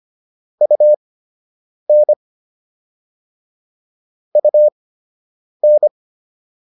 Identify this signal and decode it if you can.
Morse code, heard as a keyed tone, about 25 wpm, spaced as for 8 wpm, 600 Hz. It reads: UN UN